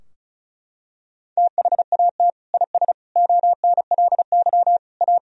Morse code, heard as a keyed tone, about 36 wpm, 700 Hz. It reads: THAT IS ONLY A
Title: Sentences from Top 1000 Words - Review Part 4 - 35wpm